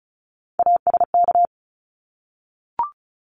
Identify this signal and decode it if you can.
Morse code, heard as a keyed tone, about 34 wpm, 700 Hz. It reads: ASK
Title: ED Suffix - Top 100 Words with Base Words 35wpm